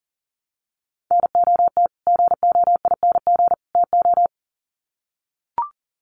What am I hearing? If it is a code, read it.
Morse code, heard as a keyed tone, about 40 wpm, 700 Hz. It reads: NOT GOING TO
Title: Sets of 3 Words 40wpm